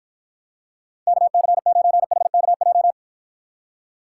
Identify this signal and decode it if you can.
Morse code, heard as a keyed tone, about 53 wpm, 700 Hz. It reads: KC9HKJ